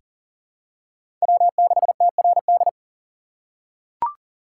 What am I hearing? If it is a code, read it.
Morse code, heard as a keyed tone, about 38 wpm, 700 Hz. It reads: W6TRD
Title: USA Call Signs - 5 Characters - 6x Character Spacing - 40wpm